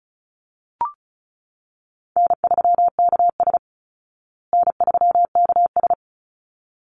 Morse code, heard as a keyed tone, about 34 wpm, 700 Hz. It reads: N3KS N3KS
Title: Easy Call Signs Repeated 2x - 35wpm